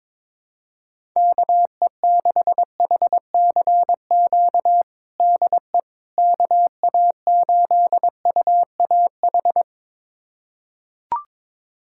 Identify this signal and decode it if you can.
Morse code, heard as a keyed tone, about 22 wpm, 700 Hz. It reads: KE6HCQ DE KA8UAH
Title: USA Callsign DE Callsign 22wpm